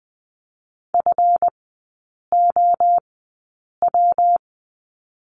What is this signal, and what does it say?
Morse code, heard as a keyed tone, about 20 wpm, spaced as for 8 wpm, 700 Hz. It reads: FOW
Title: 138 Lesson 23e - Letter f - Three Characters 20wpm